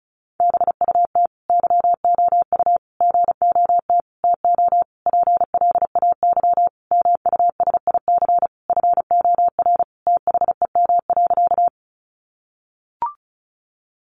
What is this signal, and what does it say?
Morse code, heard as a keyed tone, about 34 wpm, 700 Hz. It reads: BUT YOU GOT TO PLAY MUSIC FOR THEM.
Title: Sentences from Top 300 Words 35wpm